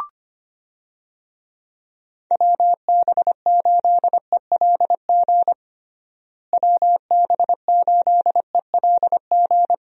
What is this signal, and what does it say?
Morse code, heard as a keyed tone, about 25 wpm, 700 Hz. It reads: WB8ELG WB8ELG
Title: USA Call Signs - 6 Characters - Repeated 2x - 25wpm